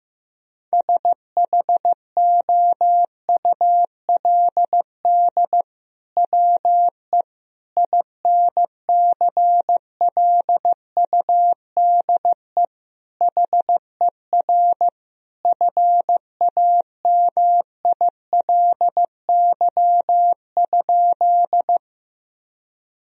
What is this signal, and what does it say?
Morse code, heard as a keyed tone, about 16 wpm, 700 Hz. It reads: SHOULD WE INCLUDE HER FAMILY?
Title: Sentences from Top 1000 Words - Review Part 3 - 15wpm